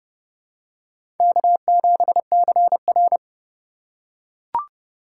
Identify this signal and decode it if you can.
Morse code, heard as a keyed tone, about 30 wpm, 700 Hz. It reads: K7CR
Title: USA Call Signs - 4 Characters - 2x Character Spacing - 30wpm